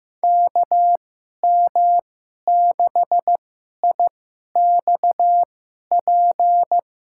Morse code, heard as a keyed tone, about 15 wpm, spaced as for 11 wpm, 700 Hz. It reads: KM6IXP
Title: USA Call Signs - 2x Character Spacing - 15wpm